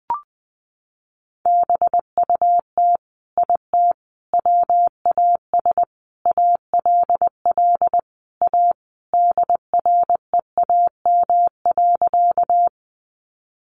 Morse code, heard as a keyed tone, about 20 wpm, 700 Hz. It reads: BUT IT WAS ALL A DREAM.
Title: Sentences from Top 1500 Words 20wpm